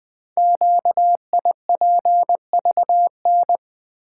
Morse code, heard as a keyed tone, about 20 wpm, 700 Hz. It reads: QIPVN
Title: Code Groups - 5 Letter:Numbers - 20wpm